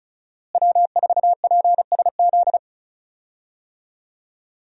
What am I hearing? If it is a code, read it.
Morse code, heard as a keyed tone, about 36 wpm, 700 Hz. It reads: W4PSZ
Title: USA Call Signs - 5 Characters - Repeated 3x - 35wpm